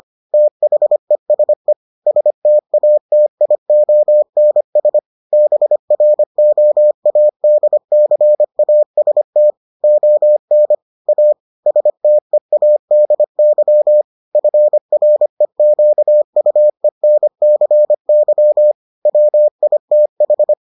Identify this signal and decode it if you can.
Morse code, heard as a keyed tone, about 25 wpm, 600 Hz. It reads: THESE STATIONS BROADCAST ON A STEADY FREQUENCY WITH